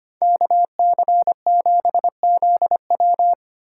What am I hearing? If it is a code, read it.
Morse code, heard as a keyed tone, about 30 wpm, 700 Hz. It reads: KC7ZW
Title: Speed Racing - USA 2x2 Call Signs 25wpm